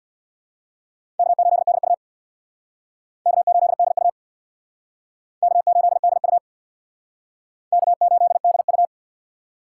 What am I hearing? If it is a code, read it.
Morse code, heard as a keyed tone, about 64 wpm, 700 Hz. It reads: K8DU K8DU K8DU K8DU